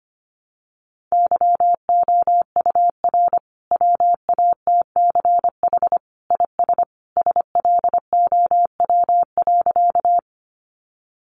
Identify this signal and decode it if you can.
Morse code, heard as a keyed tone, about 25 wpm, 700 Hz. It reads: YOUR WATCH IS SLOW.
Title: Sentences from Top 400 Words 25wpm